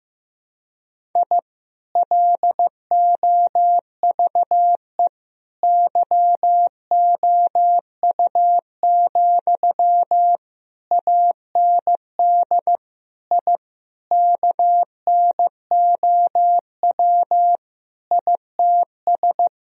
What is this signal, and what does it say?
Morse code, heard as a keyed tone, about 15 wpm, 700 Hz. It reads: I LOVE YOU, AND I KNOW ITS
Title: Sentences from Top 1000 Words - Review Part 2 - 15wpm